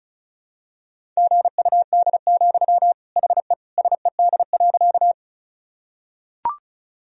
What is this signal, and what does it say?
Morse code, heard as a keyed tone, about 34 wpm, 690 Hz. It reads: GUD, HE SED.
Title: Sentences from Top 200 Words and Abbreviations 35wpm